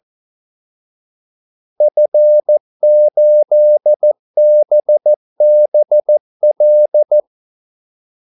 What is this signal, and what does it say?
Morse code, heard as a keyed tone, about 14 wpm, 600 Hz. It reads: F8BBL
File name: Callsigns-14wpm